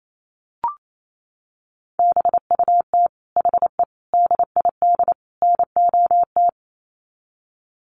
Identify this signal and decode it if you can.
Morse code, heard as a keyed tone, about 28 wpm, 700 Hz. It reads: BUT HE DID NOT
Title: Sets of 4 Words 28wpm